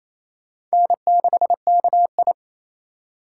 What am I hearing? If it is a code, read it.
Morse code, heard as a keyed tone, about 28 wpm, 700 Hz. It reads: N6KI
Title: USA Call Signs - 4 Characters - 3x Character Spacing - 28wpm